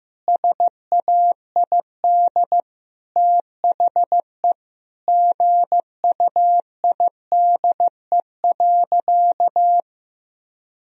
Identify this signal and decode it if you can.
Morse code, heard as a keyed tone, about 15 wpm, 700 Hz. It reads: SAID THE GUIDE.
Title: Sentences from Top 1000 Words - Review Part 3 - 15wpm